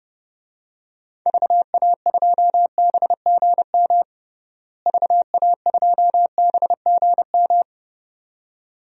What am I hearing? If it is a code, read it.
Morse code, heard as a keyed tone, about 32 wpm, 700 Hz. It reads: VA2BGM VA2BGM
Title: DXCC-100 Call Signs - 6 Characters - Repeated 2x - 30wpm